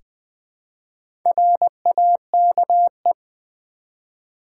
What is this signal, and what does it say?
Morse code, heard as a keyed tone, about 20 wpm, 700 Hz. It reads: RAKE
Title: Easy Words with Hints - 20wpm